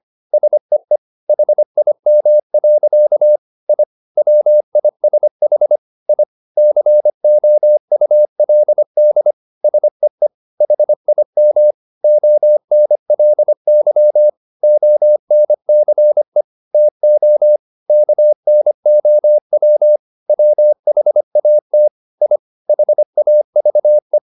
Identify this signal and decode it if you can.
Morse code, heard as a keyed tone, about 25 wpm, 600 Hz. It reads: SEE HIM. I WISH I COULD SEE HIM ONLY ONCE TO KNOW WHAT I HAVE
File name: the old man and the sea part 1 25wpm